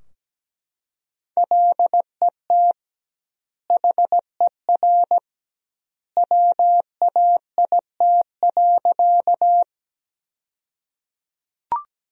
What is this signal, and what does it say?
Morse code, heard as a keyed tone, about 17 wpm, 700 Hz. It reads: LET HER WAIT.
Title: 2x Word-Spacing - Sentences from Top 500 Words - 17wpm